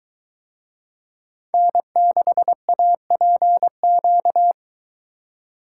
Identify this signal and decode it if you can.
Morse code, heard as a keyed tone, about 22 wpm, 700 Hz. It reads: N6APQ